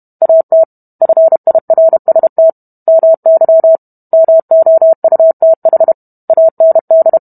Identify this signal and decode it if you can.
Morse code, heard as a keyed tone, about 32 wpm, 650 Hz. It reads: AT FIRST MY MOUTH AND